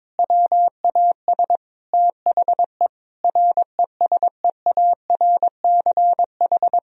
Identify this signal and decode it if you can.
Morse code, heard as a keyed tone, about 22 wpm, 700 Hz. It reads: WAS THE RESEARCH
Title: Sentences from Top 1000 Words - Review Part 4 - 22wpm